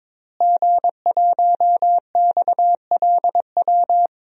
Morse code, heard as a keyed tone, about 22 wpm, 700 Hz. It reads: G1XLW
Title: DXCC-100 Call Signs - 5 Characters - Repeated 2x - 22wpm